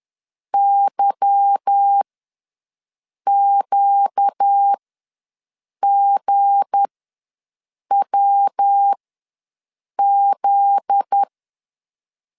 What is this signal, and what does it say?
Morse code, heard as a keyed tone, about 10 wpm, spaced as for 6 wpm, 800 Hz. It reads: YQGWZ
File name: LEZIONI / 8 YJ45 / 8 YJ ws 10wpm 4043